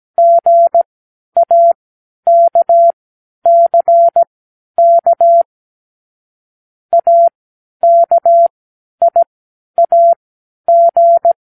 Morse code, heard as a keyed tone, about 17 wpm, spaced as for 10 wpm, 680 Hz. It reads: GAKCK AKIAG
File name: k7qo / Lesson18(A-L)